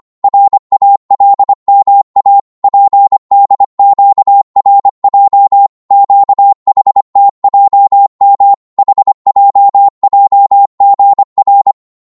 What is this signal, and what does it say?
Morse code, heard as a keyed tone, about 25 wpm, 800 Hz. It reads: RALMA PDQRJ QHTJM HJJGR
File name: Morse / ltr-25wpm